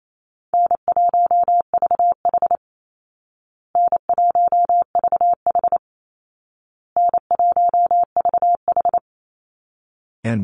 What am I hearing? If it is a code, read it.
Morse code, heard as a keyed tone, about 28 wpm, 700 Hz. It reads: N1VH N1VH N1VH
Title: Easy Call Signs Repeated 3x - 28wpm